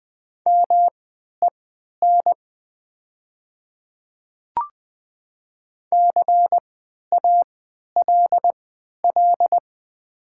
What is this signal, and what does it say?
Morse code, heard as a keyed tone, about 20 wpm, spaced as for 11 wpm, 700 Hz. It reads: MEN CALL
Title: 3x Character Spacing - Top 300 Words - 20wpm